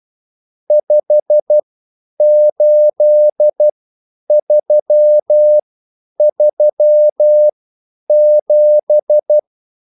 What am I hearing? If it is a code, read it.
Morse code, heard as a keyed tone, about 12 wpm, spaced as for 9 wpm, 600 Hz. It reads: 58337